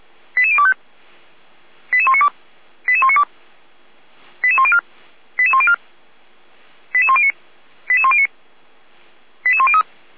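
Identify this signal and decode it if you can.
Five-tone selective call (ZVEI), heard as quick runs of five tones, tones of about 70 ms. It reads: 88025 88192 88192 88104 88104 88109 88109 88103
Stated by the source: ZVEI five-tone fire brigade alert for code 88102, received by SDR, from Feuerwehr Orlamünde.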